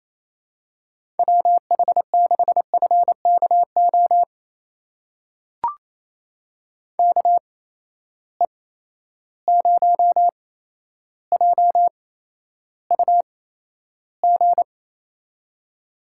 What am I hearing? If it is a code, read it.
Morse code, heard as a keyed tone, about 28 wpm, spaced as for 12 wpm, 700 Hz. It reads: WH6FKO KE0JUG
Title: USA Call Signs - 6 Characters - 8x Character Spacing - 28wpm